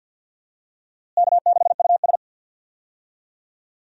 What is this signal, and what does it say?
Morse code, heard as a keyed tone, about 50 wpm, 700 Hz. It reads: K6US
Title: USA Call Signs - 4 Characters - Repeated 2x - 50wpm